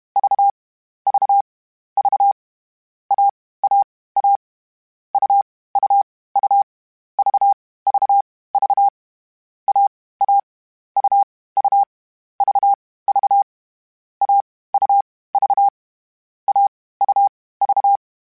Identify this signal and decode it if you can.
Morse code, heard as a keyed tone, about 31 wpm, spaced as for 17 wpm, 800 Hz. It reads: VVV AAA UUU VVV AA UU VV AUV AUV
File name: Difficult Combos / Difficult Combos-mp3s / AUV ShortRun @30